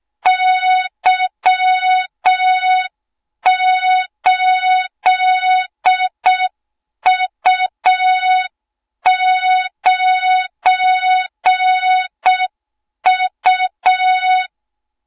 Morse code, heard as a keyed tone, about 6 wpm, 3700 Hz, with noise. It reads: Y8U9U
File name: Morze-HA2MN / hang / 20 GYAKORLAT